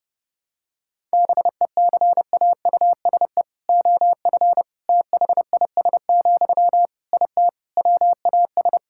Morse code, heard as a keyed tone, about 30 wpm, 700 Hz. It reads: BECAUSE OF THIS, IT WAS
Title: Sentences from Top 1000 Words - Review Part 6 - 30wpm